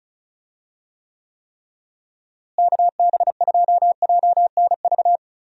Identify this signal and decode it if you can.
Morse code, heard as a keyed tone, about 36 wpm, 700 Hz. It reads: KB2JNV
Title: USA Call Signs - 6 Characters - Repeated 3x - 35wpm